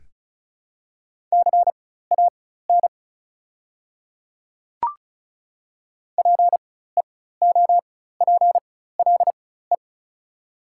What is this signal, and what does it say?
Morse code, heard as a keyed tone, about 36 wpm, spaced as for 16 wpm, 700 Hz. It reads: CAN PEOPLE
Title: 4x Character Spacing - Top 200 Words - 35wpm